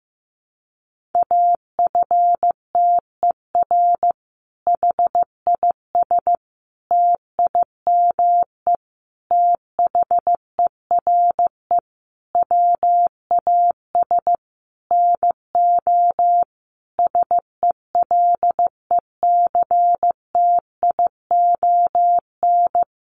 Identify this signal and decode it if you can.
Morse code, heard as a keyed tone, about 15 wpm, 700 Hz. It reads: AFTER HIS TIME THERE WAS NO SELECTION
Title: Sentences from Top 1400 Words 15wpm